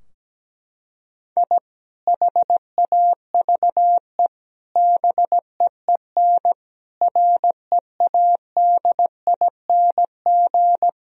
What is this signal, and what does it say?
Morse code, heard as a keyed tone, about 17 wpm, 700 Hz. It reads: I HAVE BEEN READING
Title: Sentences from Top 1000 Words - Review Part 3 - 17wpm